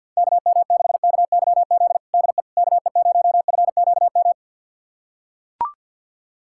Morse code, heard as a keyed tone, about 45 wpm, 700 Hz. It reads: KM6KYZ DE KE0FQM
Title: USA Callsign DE Callsign 50wpm